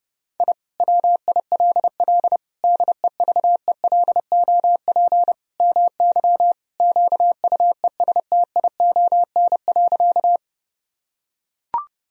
Morse code, heard as a keyed tone, about 30 wpm, 700 Hz. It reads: I WILL DEVELOP MY QUESTION.
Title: Sentences from Top 1000 Words - Review Part 2 - 30wpm